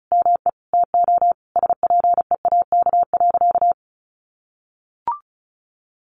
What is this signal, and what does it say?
Morse code, heard as a keyed tone, about 34 wpm, 700 Hz. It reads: ME TO SPEAK.
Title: Sentences from Top 1100 Words 35wpm